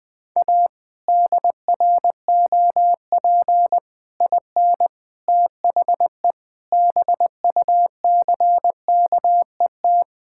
Morse code, heard as a keyed tone, about 20 wpm, 700 Hz. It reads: A DROP IN THE BUCKET
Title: Sets of 8 Words - 20wpm